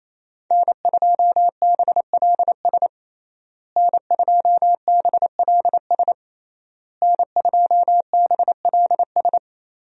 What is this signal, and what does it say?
Morse code, heard as a keyed tone, about 27 wpm, 700 Hz. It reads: N2BLS N2BLS N2BLS
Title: USA Call Signs - 5 Characters - Repeated 3x - 28wpm